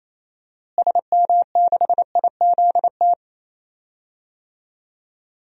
Morse code, heard as a keyed tone, about 28 wpm, 700 Hz. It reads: SM6IZT
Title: DXCC-100 Call Signs - 6 Characters - Repeated 3x - 28wpm